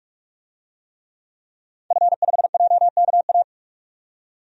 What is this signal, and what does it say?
Morse code, heard as a keyed tone, about 41 wpm, 700 Hz. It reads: F5JKA
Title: DXCC-100 Call Signs - 5 Characters - 45wpm